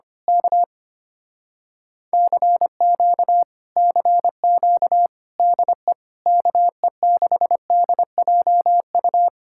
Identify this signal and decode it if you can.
Morse code, heard as a keyed tone, about 24 wpm, 700 Hz. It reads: K CQ CQ DE KE6DJU
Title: Speed-Racing Common QSO Phrases - 25wpm